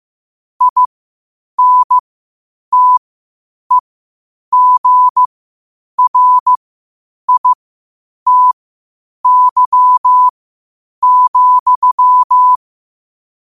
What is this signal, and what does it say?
Morse code, heard as a keyed tone, about 15 wpm, spaced as for 8 wpm, 1000 Hz. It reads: INTEGRITY,
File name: qotdfiles / 8wpm